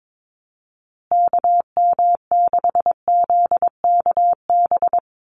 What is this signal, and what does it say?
Morse code, heard as a keyed tone, about 22 wpm, 700 Hz. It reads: KM6ZKB
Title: USA Call Signs Repeated 2x 22wpm